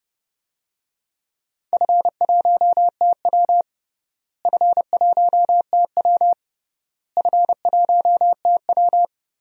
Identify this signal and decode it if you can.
Morse code, heard as a keyed tone, about 32 wpm, 700 Hz. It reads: F1TW F1TW F1TW
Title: DXCC-100 Call Signs - 4 Characters - Repeated 3x - 30wpm